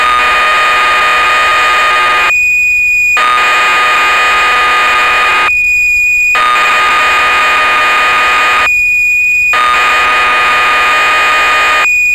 Cassette data (Kansas City type).
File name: magazines / Acorn User / 1985-07 BBC Programs